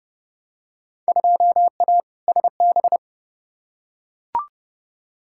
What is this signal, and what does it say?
Morse code, heard as a keyed tone, about 30 wpm, 700 Hz. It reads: 2A SB
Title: Field Day Exchanges - 30wpm